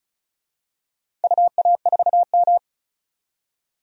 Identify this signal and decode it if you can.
Morse code, heard as a keyed tone, about 35 wpm, 700 Hz. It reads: UA4M
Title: DXCC-100 Call Signs - 4 Characters - 35wpm